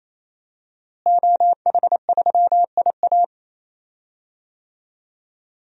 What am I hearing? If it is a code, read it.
Morse code, heard as a keyed tone, about 28 wpm, 700 Hz. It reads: OH3IA
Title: DXCC-100 Call Signs - 5 Characters - Repeated 3x - 28wpm